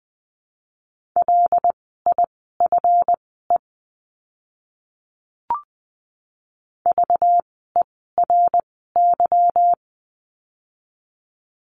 Morse code, heard as a keyed tone, about 20 wpm, spaced as for 14 wpm, 700 Hz. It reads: LIFE VERY
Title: 237 Lesson 38f - Letter q - Words 20wpm